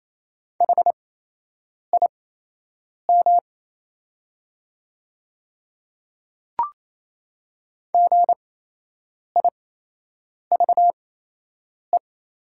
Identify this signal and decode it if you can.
Morse code, heard as a keyed tone, about 28 wpm, spaced as for 8 wpm, 700 Hz. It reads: HIM GIVE
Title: 8x Character Spacing - Top 300 Words - 28wpm